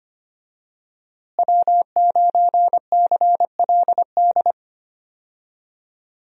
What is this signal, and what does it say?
Morse code, heard as a keyed tone, about 25 wpm, 700 Hz. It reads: W9CLD
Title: USA Call Signs - 5 Characters - Repeated 2x - 25wpm